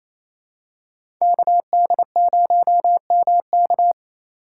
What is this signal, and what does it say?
Morse code, heard as a keyed tone, about 27 wpm, 700 Hz. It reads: KD0MK
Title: USA Call Signs - 2x Character Spacing - 28wpm